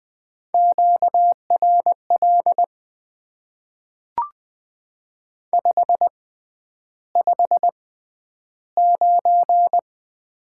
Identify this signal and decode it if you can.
Morse code, heard as a keyed tone, about 20 wpm, spaced as for 10 wpm, 700 Hz. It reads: QRL 559
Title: Top 90 QSO Elements - 6x Character Spacing - 20wpm